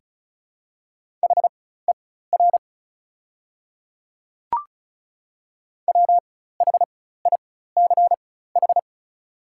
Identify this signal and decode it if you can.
Morse code, heard as a keyed tone, about 36 wpm, spaced as for 16 wpm, 700 Hz. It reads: HER WHICH
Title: 4x Character Spacing - Top 100 Words - 35wpm